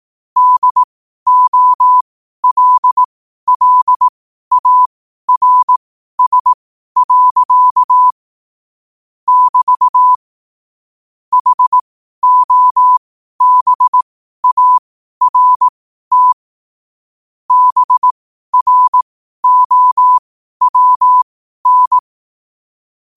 Morse code, heard as a keyed tone, about 18 wpm, spaced as for 12 wpm, 1000 Hz. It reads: DOLLARS. <BT> HOBART BROWN
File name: qotdfiles / 12wpm